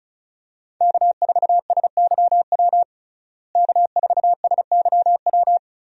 Morse code, heard as a keyed tone, about 34 wpm, 700 Hz. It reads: K4SYW K4SYW
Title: USA Call Signs - 5 Characters - Repeated 3x - 35wpm